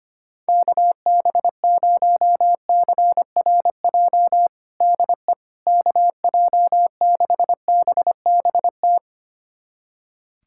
Morse code, heard as a keyed tone, about 25 wpm, 700 Hz. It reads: KB0CRJ DE KJ6BBT
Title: USA Callsign DE Callsign - 6 Characters - 25wpm